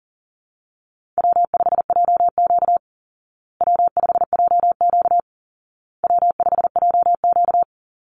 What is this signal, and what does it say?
Morse code, heard as a keyed tone, about 40 wpm, 700 Hz. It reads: W5JQ W5JQ W5JQ
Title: Call Signs Repeated 2x - 40wpm